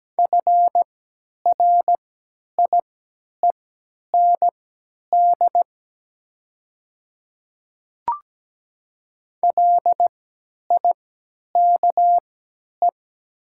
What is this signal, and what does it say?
Morse code, heard as a keyed tone, about 17 wpm, spaced as for 10 wpm, 690 Hz. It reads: FRIEND LIKE
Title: 3x Character Spacing - Top 300 Words - 17wpm